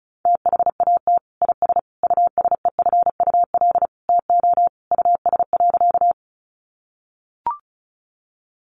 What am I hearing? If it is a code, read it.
Morse code, heard as a keyed tone, about 34 wpm, 700 Hz. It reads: THAT IS USEFUL TO US.
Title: Sentences from Top 700 Words 35wpm